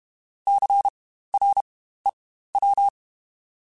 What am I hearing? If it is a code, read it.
Morse code, heard as a keyed tone, about 32 wpm, spaced as for 15 wpm, 800 Hz. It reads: CREW
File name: Spaced Alphas / Spaced Alphas mp3s / Cs sp @30